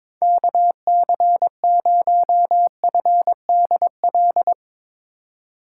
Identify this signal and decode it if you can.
Morse code, heard as a keyed tone, about 23 wpm, 700 Hz. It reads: KC0FDL
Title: USA Call Signs - 4x Character Spacing - 22wpm